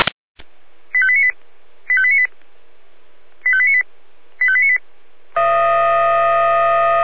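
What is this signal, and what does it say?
Five-tone selective call (ZVEI), heard as quick runs of five tones, tones of about 70 ms. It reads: 85898 85898 85898 85898
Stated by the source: SDR receiver, ZVEI five-tone selective call, address 85310.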